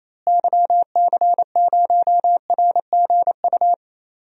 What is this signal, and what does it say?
Morse code, heard as a keyed tone, about 28 wpm, 700 Hz. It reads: YC0RGU
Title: DXCC-100 Call Signs - 6 Characters - Repeated 2x - 28wpm